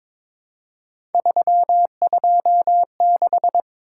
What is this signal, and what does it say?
Morse code, heard as a keyed tone, about 22 wpm, 700 Hz. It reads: 326
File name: practice / 22wpm / Three Letter-Number Combinations - 22wpm